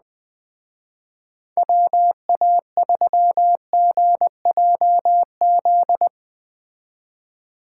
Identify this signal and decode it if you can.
Morse code, heard as a keyed tone, about 20 wpm, 700 Hz. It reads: WA3GJZ
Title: Speed Racing - USA Call Signs 17wpm